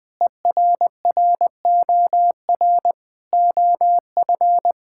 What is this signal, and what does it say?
Morse code, heard as a keyed tone, about 20 wpm, 700 Hz. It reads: ERROR OF
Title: Sets of 8 Words - 20wpm